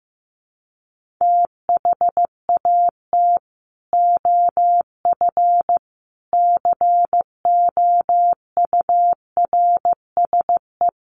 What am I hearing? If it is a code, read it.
Morse code, heard as a keyed tone, about 15 wpm, 700 Hz. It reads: THAT OF COURSE
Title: Sentences from Top 1500 Words 15wpm